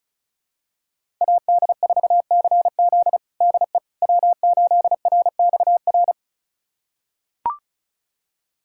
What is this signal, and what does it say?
Morse code, heard as a keyed tone, about 34 wpm, 700 Hz. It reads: AD4CZ DE W8RXR
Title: USA Callsign DE Callsign - 5 Characters - 35wpm